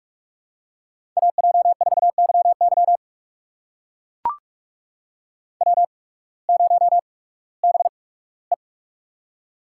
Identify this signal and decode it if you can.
Morse code, heard as a keyed tone, about 44 wpm, spaced as for 21 wpm, 700 Hz. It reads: AJ4YY W0BE
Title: USA Call Signs - 8x Character Spacing - 45wpm